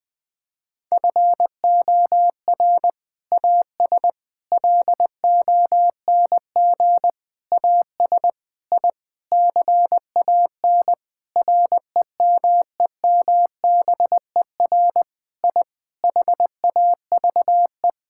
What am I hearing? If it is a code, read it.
Morse code, heard as a keyed tone, about 21 wpm, 700 Hz. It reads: FOR AS LONG AS I CAN REMEMBER I HAVE
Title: Sentences with 8-Word Phrases - 20wpm